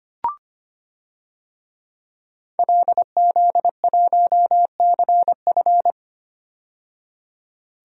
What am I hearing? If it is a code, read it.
Morse code, heard as a keyed tone, about 25 wpm, 700 Hz. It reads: LZ1CF
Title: DXCC-100 Call Signs - 5 Characters - 25wpm